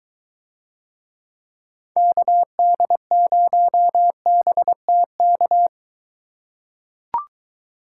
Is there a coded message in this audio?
Yes: Morse code from 2.0 to 5.7 s, heard as a keyed tone, about 23 wpm, 700 Hz. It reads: KD0BTK